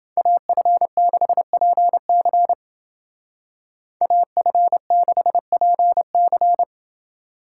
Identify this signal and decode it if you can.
Morse code, heard as a keyed tone, about 28 wpm, 700 Hz. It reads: AF6PC AF6PC